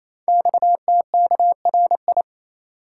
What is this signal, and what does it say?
Morse code, heard as a keyed tone, about 28 wpm, 700 Hz. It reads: XTKRI
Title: Code Groups - 5 Letters - 28wpm